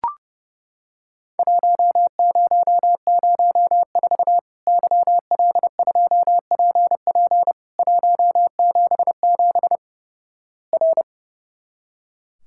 Morse code, heard as a keyed tone, about 29 wpm, 700 Hz. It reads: 1004 YL2PP 177 R
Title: Worked All Europe DX Contest - QTC Exchange - 30wpm